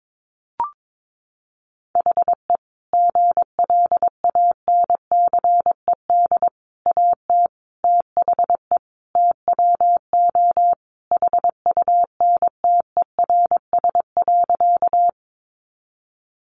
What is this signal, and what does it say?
Morse code, heard as a keyed tone, about 22 wpm, 700 Hz. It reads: HE GLANCED AT THE TWO HUNTERS.
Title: A Sound Of Thunder 22wpm